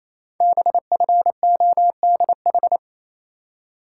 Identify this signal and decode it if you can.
Morse code, heard as a keyed tone, about 28 wpm, 700 Hz. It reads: BFODH
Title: Code Groups - 5 Letters - 28wpm